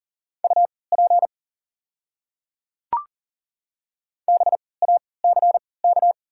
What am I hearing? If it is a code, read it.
Morse code, heard as a keyed tone, about 40 wpm, spaced as for 23 wpm, 700 Hz. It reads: UP BACK
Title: 3x Character Spacing - Top 200 Words - 40wpm